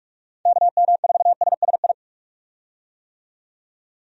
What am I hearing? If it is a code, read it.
Morse code, heard as a keyed tone, about 43 wpm, 700 Hz. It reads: KM4SSI
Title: USA Call Signs - 6 Characters - 45wpm